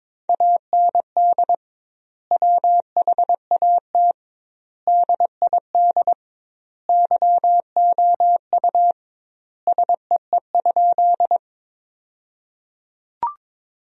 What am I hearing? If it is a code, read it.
Morse code, heard as a keyed tone, about 21 wpm, 700 Hz. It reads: AND WHAT DID YOU SEE?
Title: 2x Word-Spacing - Sentences from Top 100 Words - 22wpm